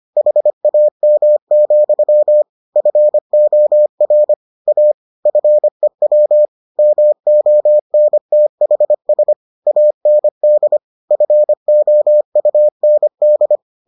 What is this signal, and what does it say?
Morse code, heard as a keyed tone, about 25 wpm, 600 Hz. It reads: HAM, FOR A FEW MONTHS AND FOUND